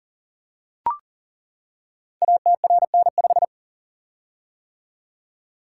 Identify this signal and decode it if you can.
Morse code, heard as a keyed tone, about 39 wpm, 700 Hz. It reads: ATRN5
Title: Code Groups - 5 Letter:Numbers - 40wpm